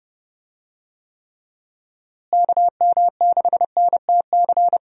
Morse code, heard as a keyed tone, about 31 wpm, 700 Hz. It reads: KM6NTC
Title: USA Call Signs - 6 Characters - Repeated 3x - 30wpm